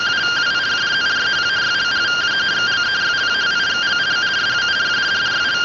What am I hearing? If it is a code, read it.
Two-tone teleprinter signal, about 170 Hz shift.
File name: findings / axial iodine 3mm 205.gen-38 rtty rx